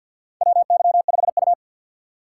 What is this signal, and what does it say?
Morse code, heard as a keyed tone, about 50 wpm, 700 Hz. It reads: WY5U
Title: USA Call Signs - 4 Characters - 3x Character Spacing - 50wpm